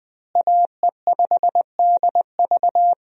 Morse code, heard as a keyed tone, about 20 wpm, 700 Hz. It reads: AE5DV
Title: USA Call Signs - 5 Characters - Repeated 2x - 20wpm